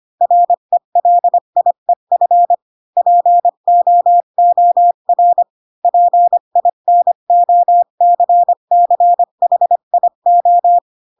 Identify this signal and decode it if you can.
Morse code, heard as a keyed tone, about 25 wpm, 700 Hz. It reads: RELIEF POOR PINOCCHIO
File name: Pinocchio / Pinocchio ch5 20 25 ik1hge